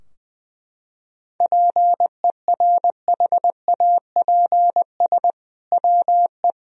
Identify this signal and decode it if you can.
Morse code, heard as a keyed tone, about 20 wpm, 700 Hz. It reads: PERHAPS WE
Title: Sentences from Top 1000 Words - Review Part 4 - 20wpm